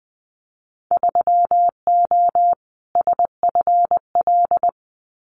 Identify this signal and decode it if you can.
Morse code, heard as a keyed tone, about 20 wpm, 700 Hz. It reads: 3O SFL
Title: Winter Field Day Exchanges 20wpm